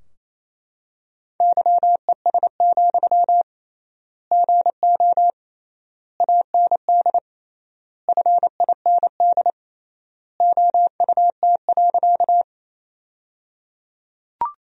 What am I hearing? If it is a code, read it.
Morse code, heard as a keyed tone, about 28 wpm, 700 Hz. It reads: YES, GO AND FIND OUT.
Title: 3x Word-Spacing - Sentences from Top 500 Words - 28wpm